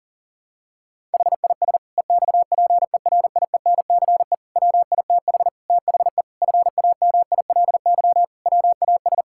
Morse code, heard as a keyed tone, about 37 wpm, 700 Hz. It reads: HIS EXPERIENCE WITH THE FAMILY WAS
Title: Sentences from Top 1000 Words - Review Part 6 - 40wpm